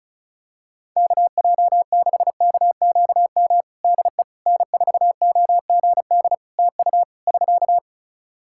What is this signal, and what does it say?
Morse code, heard as a keyed tone, about 36 wpm, 700 Hz. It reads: KJ6KQM DE N4OGD TU <SK>
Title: Practice QSO - Exchange 4 - Ending - 35wpm